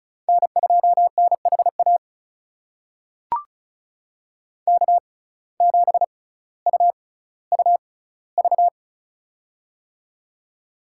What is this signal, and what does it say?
Morse code, heard as a keyed tone, about 34 wpm, spaced as for 18 wpm, 700 Hz. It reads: N2NHA K7UUV
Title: USA Call Signs - 5 Characters - 6x Character Spacing - 35wpm